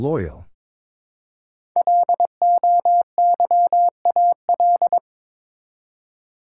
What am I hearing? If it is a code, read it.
Morse code, heard as a keyed tone, about 22 wpm, 700 Hz. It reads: LOYAL
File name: 5 Letter Words - First 3 Letters Given - 22wpm